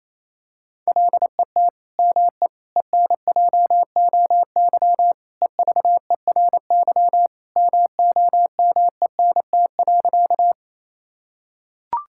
Morse code, heard as a keyed tone, about 28 wpm, 700 Hz. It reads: LET ME ENJOY EVERY MOMENT.
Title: Sentences from Top 1000 Words - Review Part 5 - 28wpm